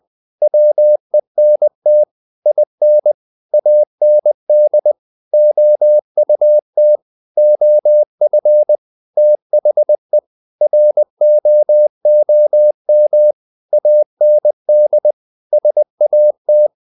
Morse code, heard as a keyed tone, about 20 wpm, 600 Hz. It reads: WENT IN AND OUT OF THE ROOM AND SAT
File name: the old man and the sea part 2 20wpm